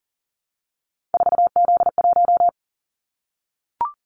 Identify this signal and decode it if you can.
Morse code, heard as a keyed tone, about 41 wpm, 700 Hz. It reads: 4Z1
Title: Callsign Prefixes for Top 50 DXCC - 40wpm